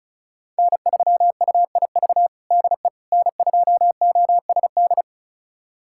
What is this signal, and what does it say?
Morse code, heard as a keyed tone, about 36 wpm, 700 Hz. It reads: N3UIV DE N2OSD
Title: USA Callsign DE Callsign - 5 Characters - 35wpm